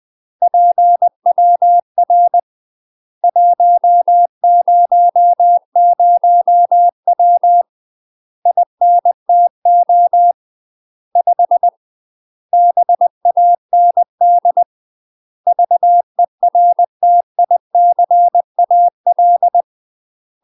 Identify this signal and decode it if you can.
Morse code, heard as a keyed tone, about 20 wpm, 700 Hz. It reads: PWR 100W INTO 5 BAND VERTICAL
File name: UK MASTER QSOs TRAINING/20WPM